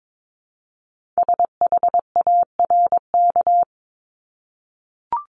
Morse code, heard as a keyed tone, about 22 wpm, 700 Hz. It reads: SHARK
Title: Multi-letter Phonemes Repeated 3x 22wpm